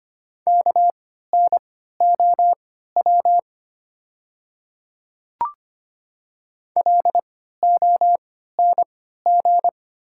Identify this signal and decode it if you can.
Morse code, heard as a keyed tone, about 25 wpm, spaced as for 14 wpm, 690 Hz. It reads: KNOW LONG
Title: 3x Character Spacing - Top 100 Words - 25wpm